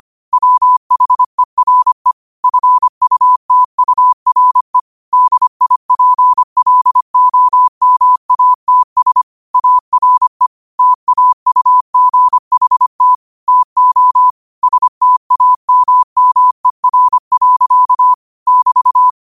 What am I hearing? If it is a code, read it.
Morse code, heard as a keyed tone, about 25 wpm, 1000 Hz. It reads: WHERE FUTURE DIPLOMATS ARE TAUGHT TO STAMMER. <BT>
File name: 25wpm